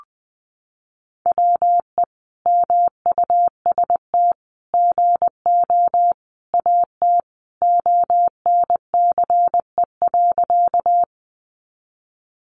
Morse code, heard as a keyed tone, about 20 wpm, 700 Hz. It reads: WE MUST GO AT ONCE.
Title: Sentences from Top 300 Words 20wpm